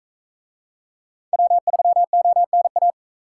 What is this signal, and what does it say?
Morse code, heard as a keyed tone, about 44 wpm, 700 Hz. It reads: W3ONA